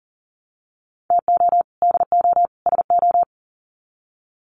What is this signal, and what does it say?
Morse code, heard as a keyed tone, about 40 wpm, 700 Hz. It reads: TO DO SO
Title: Sets of 3 Words 40wpm